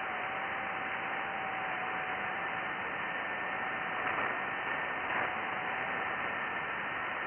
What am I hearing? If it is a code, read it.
FT8.
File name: IC-7100